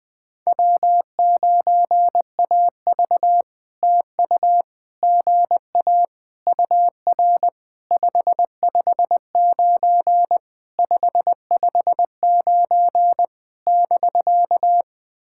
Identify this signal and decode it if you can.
Morse code, heard as a keyed tone, about 19 wpm, 700 Hz. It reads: W9AV TU GA UR 559 559 <BK>
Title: POTA QSO Practice - 20wpm